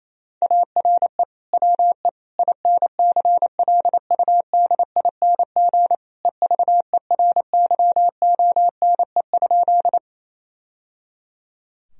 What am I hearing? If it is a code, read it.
Morse code, heard as a keyed tone, about 27 wpm, 700 Hz. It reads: ARE WE INCLUDING EVERYONE?
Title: Sentences from Top 1000 Words - Review Part 4 - 28wpm